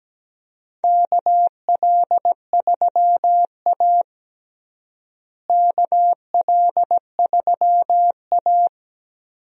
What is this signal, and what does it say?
Morse code, heard as a keyed tone, about 17 wpm, 700 Hz. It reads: KL3A KL3A
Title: USA Call Signs - 4 Characters - Repeated 2x - 17wpm